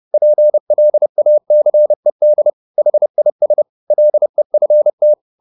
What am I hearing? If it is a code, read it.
Morse code, heard as a keyed tone, about 30 wpm, 600 Hz. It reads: PLACED HIS LEFT